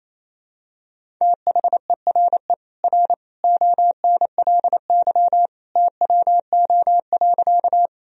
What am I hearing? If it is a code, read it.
Morse code, heard as a keyed tone, about 28 wpm, 700 Hz. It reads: THERE R ONLY TWO.
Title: Sentences from Top 200 Words and Abbreviations 28wpm